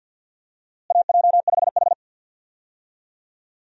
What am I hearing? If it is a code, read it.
Morse code, heard as a keyed tone, about 52 wpm, 700 Hz. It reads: AJ5H
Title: USA Call Signs - 4 Characters - Repeated 2x - 50wpm